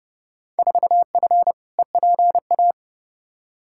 Morse code, heard as a keyed tone, about 32 wpm, 700 Hz. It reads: 4F EPA
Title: Field Day Exchanges Encore - 30wpm